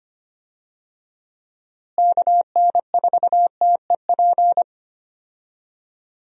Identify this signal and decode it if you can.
Morse code, heard as a keyed tone, about 25 wpm, 700 Hz. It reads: KN4TEP
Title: USA Call Signs - 6 Characters - Repeated 3x - 25wpm